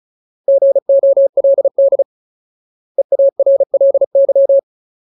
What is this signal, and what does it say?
Morse code, heard as a keyed tone, about 34 wpm, 550 Hz. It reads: GOLD EARLY
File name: Ultimate Word List / 35WPM Ultimate